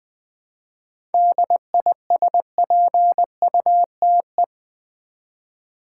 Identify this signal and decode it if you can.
Morse code, heard as a keyed tone, about 20 wpm, 700 Hz. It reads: DISPUTE
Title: 7 Letter Words - First 3 Letters Given - 20wpm